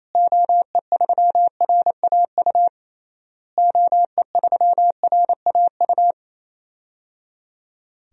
Morse code, heard as a keyed tone, about 28 wpm, 700 Hz. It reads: OE3RAU OE3RAU
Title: DXCC-100 Call Signs - 6 Characters - Repeated 3x - 28wpm